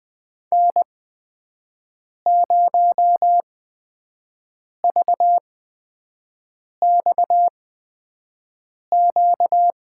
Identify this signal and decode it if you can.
Morse code, heard as a keyed tone, about 21 wpm, spaced as for 5 wpm, 700 Hz. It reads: N0VXQ
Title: USA Call Signs - 5 Characters - 8x Character Spacing - 20wpm